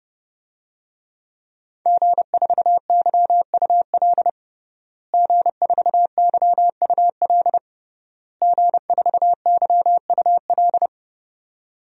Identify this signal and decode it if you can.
Morse code, heard as a keyed tone, about 32 wpm, 700 Hz. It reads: G4YUL G4YUL G4YUL
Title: DXCC-100 Call Signs - 5 Characters - Repeated 3x - 30wpm